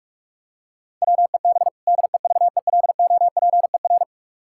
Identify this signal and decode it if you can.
Morse code, heard as a keyed tone, about 46 wpm, 700 Hz. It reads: WEB DEVELOPER
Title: K1USN SST QSO Practice 45wpm